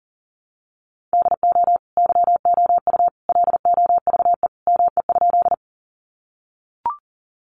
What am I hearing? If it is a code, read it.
Morse code, heard as a keyed tone, about 40 wpm, 700 Hz. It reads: DO YOU LOVE ME?
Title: Sentences from Top 400 Words 40wpm